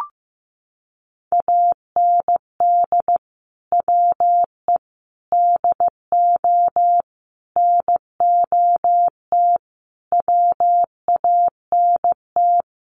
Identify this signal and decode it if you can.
Morse code, heard as a keyed tone, about 15 wpm, 700 Hz. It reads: AND WE DO NOT WANT
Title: Sets of 5 Words Encore2 15wpm